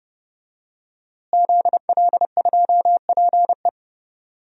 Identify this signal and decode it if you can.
Morse code, heard as a keyed tone, about 30 wpm, 700 Hz. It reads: ZL2PE
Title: DXCC-100 Call Signs - 5 Characters - Repeated 3x - 30wpm